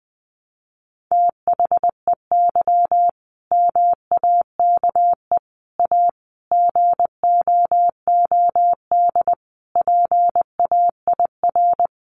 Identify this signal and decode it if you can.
Morse code, heard as a keyed tone, about 20 wpm, 700 Hz. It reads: THEY MAKE A GOOD PAIR
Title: Sentences from Top 1200 Words 20wpm